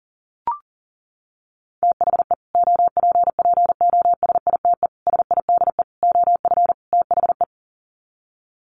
Morse code, heard as a keyed tone, about 40 wpm, 700 Hz. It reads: THE OPPOSITE SIDE OF THE
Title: Sets of 5 Words Encore 40wpm